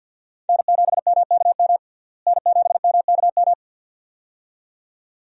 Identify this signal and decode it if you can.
Morse code, heard as a keyed tone, about 51 wpm, 700 Hz. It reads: N7MKM N7MKM
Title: USA Call Signs - 5 Characters - Repeated 2x - 50wpm